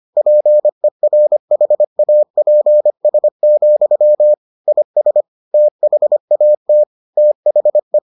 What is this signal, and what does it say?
Morse code, heard as a keyed tone, about 25 wpm, 600 Hz. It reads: PERHAPS, IS THAT THE